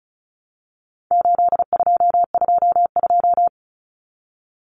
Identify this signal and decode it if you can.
Morse code, heard as a keyed tone, about 34 wpm, 700 Hz. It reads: 8222
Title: Common Cut Numbers - 4-Digit 35wpm